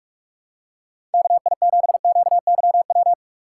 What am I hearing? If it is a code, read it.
Morse code, heard as a keyed tone, about 48 wpm, 700 Hz. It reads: KI7QYW